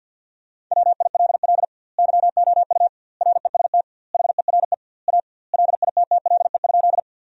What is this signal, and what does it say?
Morse code, heard as a keyed tone, about 53 wpm, 700 Hz. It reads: WILL YOU REST HERE A LITTLE?
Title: 2x Word-Spacing - Sentences from Top 500 Words - 50wpm